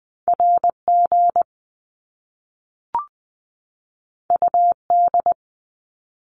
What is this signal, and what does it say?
Morse code, heard as a keyed tone, about 20 wpm, 700 Hz. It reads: RG UD
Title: Top Two Letter Combinations in Words 20wpm